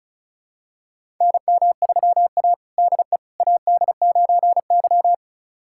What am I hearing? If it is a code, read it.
Morse code, heard as a keyed tone, about 36 wpm, 700 Hz. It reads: NM3A DE AD9Y
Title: USA Callsign DE Callsign - 4 Characters - 35wpm